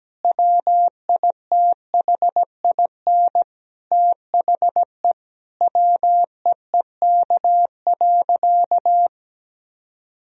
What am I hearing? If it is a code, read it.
Morse code, heard as a keyed tone, about 17 wpm, 700 Hz. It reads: WITHIN THE WEEK.
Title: Sentences from Top 1000 Words - Review Part 4 - 17wpm